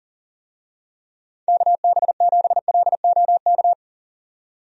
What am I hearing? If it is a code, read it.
Morse code, heard as a keyed tone, about 40 wpm, 700 Hz. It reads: KB7LOK
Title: USA Call Signs - 6 Characters - 40wpm